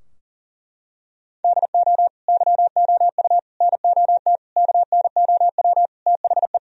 Morse code, heard as a keyed tone, about 37 wpm, 700 Hz. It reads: DO YOU NOT KNOW THE
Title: Sentences from Top 1000 Words - Review Part 3 - 40wpm